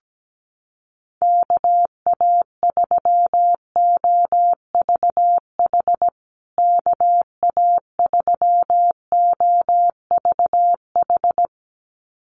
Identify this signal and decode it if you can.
Morse code, heard as a keyed tone, about 17 wpm, 700 Hz. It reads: KA3OVH KA3OVH
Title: USA Call Signs Repeated 2x 17wpm